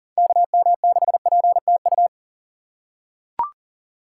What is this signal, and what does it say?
Morse code, heard as a keyed tone, about 37 wpm, 700 Hz. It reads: KM6PTU
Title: USA Call Signs - 6 Characters - 2x Character Spacing - 40wpm